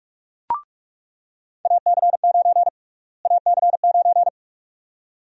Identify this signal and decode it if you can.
Morse code, heard as a keyed tone, about 46 wpm, 700 Hz. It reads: AC9 AC9
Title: USA Call Sign Prefixes - Repeated 2x - 45wpm